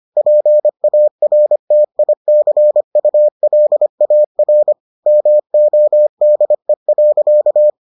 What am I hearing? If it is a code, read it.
Morse code, heard as a keyed tone, about 25 wpm, 600 Hz. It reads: PARTICULAR MODE.